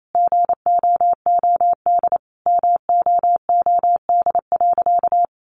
Morse code, heard as a keyed tone, about 28 wpm, 700 Hz. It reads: GOOD MOOD.
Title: Sentences from Top 1100 Words 28wpm